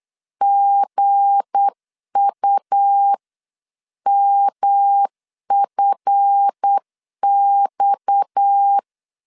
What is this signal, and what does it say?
Morse code, heard as a keyed tone, about 8 wpm, 800 Hz. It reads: GUMFX
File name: CW / DET / 1 9 1500C S A 8wpm 5637